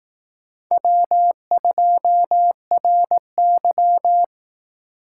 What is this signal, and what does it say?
Morse code, heard as a keyed tone, about 18 wpm, 700 Hz. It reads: W2RY
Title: Speed Racing - USA 1x2 Call Signs 15wpm